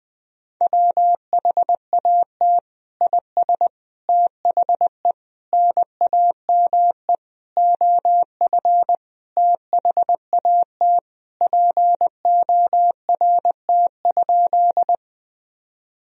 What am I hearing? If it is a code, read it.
Morse code, heard as a keyed tone, about 21 wpm, 700 Hz. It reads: WHAT IS THE NAME OF THAT PORT?
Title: Sentences from Top 1000 Words - Review Part 1 - 20wpm